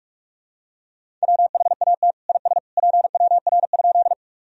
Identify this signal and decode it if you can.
Morse code, heard as a keyed tone, about 42 wpm, 700 Hz. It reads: WHAT IS PWR?
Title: Sentences from Top 500 Words and Abbreviations 45wpm